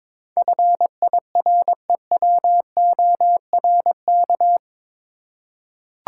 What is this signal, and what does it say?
Morse code, heard as a keyed tone, about 22 wpm, 700 Hz. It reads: FIREWORK
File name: practice / 22wpm / Eight Letter Words - 22wpm